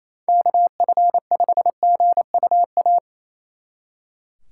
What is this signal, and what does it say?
Morse code, heard as a keyed tone, about 33 wpm, 700 Hz. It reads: KF5GUA
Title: Speed Racing - USA Call Signs 28wpm